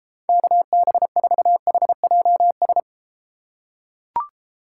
Morse code, heard as a keyed tone, about 31 wpm, 700 Hz. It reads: KB4HJS